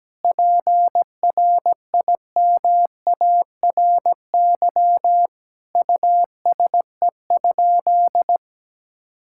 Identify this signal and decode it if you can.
Morse code, heard as a keyed tone, about 17 wpm, 700 Hz. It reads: PRIMARY USE?
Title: Sentences from Top 1000 Words - Review Part 5 - 17wpm